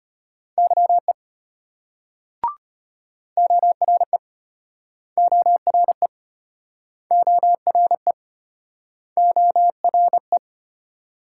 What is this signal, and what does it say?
Morse code, heard as a keyed tone, about 34 wpm, 700 Hz. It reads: YE ORE ORE ORE ORE